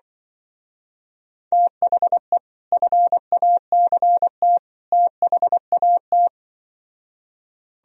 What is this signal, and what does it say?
Morse code, heard as a keyed tone, about 24 wpm, 700 Hz. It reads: THE FACT THAT
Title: Speed-Racing - Sets of 3 Words 20wpm